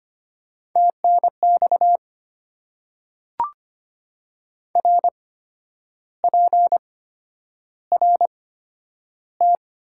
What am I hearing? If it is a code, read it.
Morse code, heard as a keyed tone, about 25 wpm, spaced as for 10 wpm, 700 Hz. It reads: TNX RPRT
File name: Top 90 QSO Elements - 8x Character Spacing - 25wpm